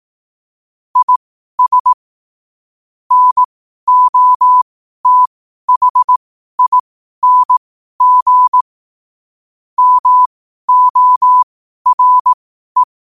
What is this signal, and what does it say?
Morse code, heard as a keyed tone, about 18 wpm, spaced as for 12 wpm, 1000 Hz. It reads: IS NOTHING MORE
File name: qotdfiles / 12wpm